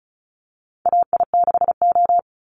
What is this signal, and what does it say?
Morse code, heard as a keyed tone, about 34 wpm, 700 Hz. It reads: AI6O
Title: Call Signs Repeated 2x - 35wpm